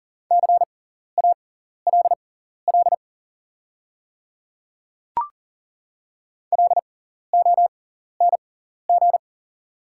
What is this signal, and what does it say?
Morse code, heard as a keyed tone, about 38 wpm, spaced as for 14 wpm, 690 Hz. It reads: CALL LONG
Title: 6x Character Spacing - Top 100 Words - 40wpm